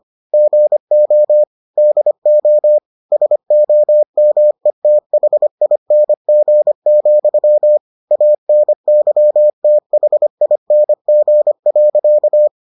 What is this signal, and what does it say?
Morse code, heard as a keyed tone, about 25 wpm, 600 Hz. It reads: GO DO SOMETHING, ANYTHING.